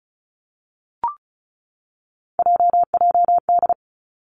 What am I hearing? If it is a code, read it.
Morse code, heard as a keyed tone, about 36 wpm, 700 Hz. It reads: JJD
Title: Three Challenging Letter-Number Combinations 35wpm